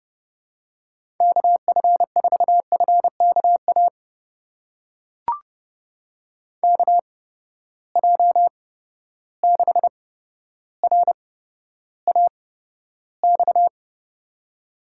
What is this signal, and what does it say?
Morse code, heard as a keyed tone, about 30 wpm, spaced as for 13 wpm, 700 Hz. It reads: KF4FKA KJ6RAX
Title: USA Call Signs - 6 Characters - 8x Character Spacing - 30wpm